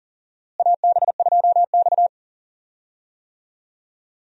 Morse code, heard as a keyed tone, about 40 wpm, 700 Hz. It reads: AB2X
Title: USA Call Signs - 4 Characters - Repeated 2x - 40wpm